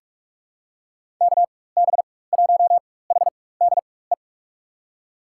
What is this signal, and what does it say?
Morse code, heard as a keyed tone, about 46 wpm, spaced as for 21 wpm, 700 Hz. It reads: KB1HDE
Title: USA Call Signs - 6 Characters - 4x Character Spacing - 45wpm